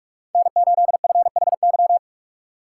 Morse code, heard as a keyed tone, about 46 wpm, 700 Hz. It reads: N8FHY
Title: USA Call Signs - 5 Characters - 3x Character Spacing - 45wpm